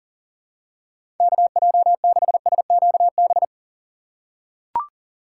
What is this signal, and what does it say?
Morse code, heard as a keyed tone, about 37 wpm, 700 Hz. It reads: KJ6SQB
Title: USA Call Signs - 6 Characters - 6x Character Spacing - 40wpm